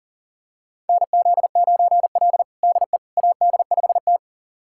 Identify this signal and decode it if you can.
Morse code, heard as a keyed tone, about 40 wpm, 700 Hz. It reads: NZ9L DE AD5T
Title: USA Callsign DE Callsign - 4 Characters - 40wpm